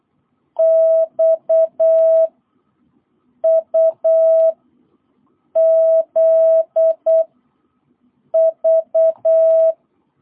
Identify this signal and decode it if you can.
Morse code, heard as a keyed tone, about 8 wpm, 650 Hz. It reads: X U Z V
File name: jh8cbh / 2022cw